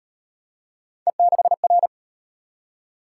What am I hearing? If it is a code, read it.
Morse code, heard as a keyed tone, about 35 wpm, 700 Hz. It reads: E6R